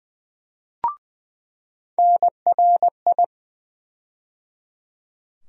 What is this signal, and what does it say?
Morse code, heard as a keyed tone, about 20 wpm, 700 Hz. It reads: NRI
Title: USA Call Sign Postfixes - 3 Letters 20wpm